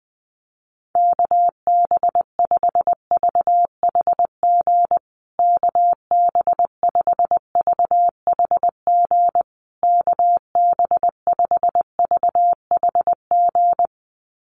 Morse code, heard as a keyed tone, about 20 wpm, 700 Hz. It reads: KB5VHG KB5VHG KB5VHG
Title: USA Call Signs Repeated 3x 20wpm